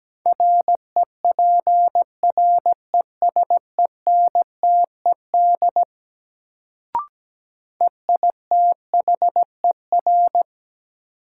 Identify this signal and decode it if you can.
Morse code, heard as a keyed tone, about 17 wpm, 700 Hz. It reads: REPRESENTED EITHER
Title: Sets of 5 Words - New Word Drill - 17wpm